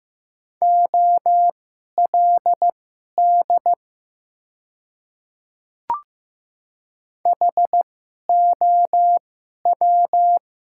Morse code, heard as a keyed tone, about 15 wpm, spaced as for 11 wpm, 700 Hz. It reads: OLD HOW
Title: 2x Character Spacing - Top 300 Words - 15wpm